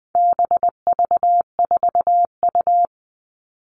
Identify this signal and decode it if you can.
Morse code, heard as a keyed tone, about 20 wpm, 700 Hz. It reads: BV4U
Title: All Cut Numbers - 4-Digit 20wpm